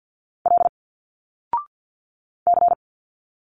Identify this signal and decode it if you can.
Morse code, heard as a keyed tone, about 50 wpm, 700 Hz. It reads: L C
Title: Single Letters 50wpm